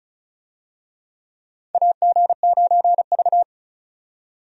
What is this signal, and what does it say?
Morse code, heard as a keyed tone, about 34 wpm, 700 Hz. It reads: AG9V
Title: USA Call Signs - 4 Characters - 35wpm